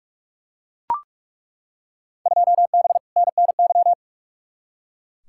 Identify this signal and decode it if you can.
Morse code, heard as a keyed tone, about 43 wpm, 700 Hz. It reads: 2B NNY
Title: Field Day Exchanges Encore - 45wpm